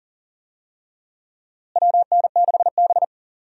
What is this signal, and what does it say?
Morse code, heard as a keyed tone, about 40 wpm, 700 Hz. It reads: WN6B
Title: USA Call Signs - 4 Characters - Repeated 2x - 40wpm